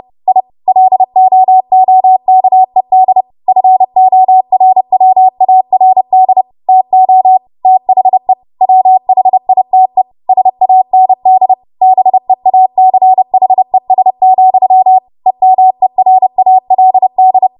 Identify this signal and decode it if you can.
Morse code, heard as a keyed tone, about 30 wpm, 750 Hz. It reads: I LOOKED FORWARD TO THE WHITE SAND BEACHES, EMERALD
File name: Morse / 30WPM